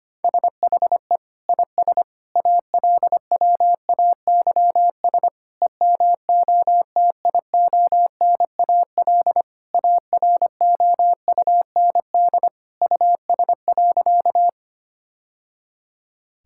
Morse code, heard as a keyed tone, about 25 wpm, 700 Hz. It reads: SHE IS ALWAYS EMOTIONAL AROUND US.
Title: Sentences from Top 1000 Words - Review Part 5 - 25wpm